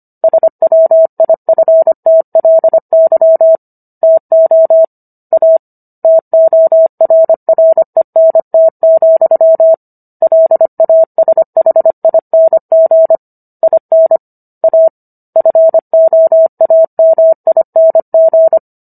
Morse code, heard as a keyed tone, about 25 wpm, 650 Hz. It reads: SWIFTLY TO A TORRENT, LASHING IN A FOAMING